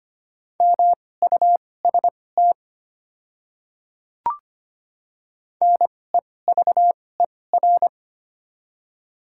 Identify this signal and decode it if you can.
Morse code, heard as a keyed tone, about 25 wpm, spaced as for 18 wpm, 700 Hz. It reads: MUST NEVER
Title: 2x Character Spacing - Top 300 Words - 25wpm